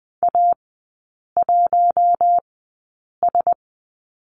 Morse code, heard as a keyed tone, about 20 wpm, spaced as for 8 wpm, 700 Hz. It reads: A1S
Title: 081 Lesson 14e - Number 2 - Three Characters 20wpm